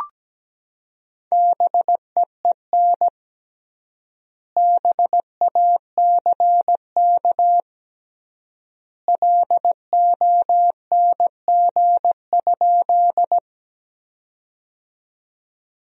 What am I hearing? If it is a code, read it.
Morse code, heard as a keyed tone, about 17 wpm, 700 Hz. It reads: BEEN BACK LONG?
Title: 3x Word-Spacing - Sentences from Top 200 Words - 17wpm